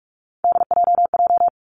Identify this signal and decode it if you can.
Morse code, heard as a keyed tone, about 46 wpm, 700 Hz. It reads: DJJ
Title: Three Challenging Letter-Number Combinations 45wpm